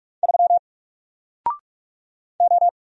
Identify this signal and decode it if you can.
Morse code, heard as a keyed tone, about 44 wpm, 700 Hz. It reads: 3 O
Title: Single Letter-Number Alt 45wpm